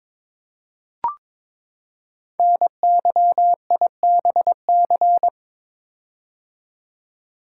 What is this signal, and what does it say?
Morse code, heard as a keyed tone, about 21 wpm, 700 Hz. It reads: NYIBC
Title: Code Groups - 5 Letter:Numbers - 22wpm